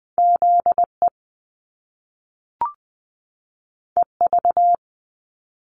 Morse code, heard as a keyed tone, about 20 wpm, 700 Hz. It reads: Z E E V
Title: Top Two Letter Combinations in Words 20wpm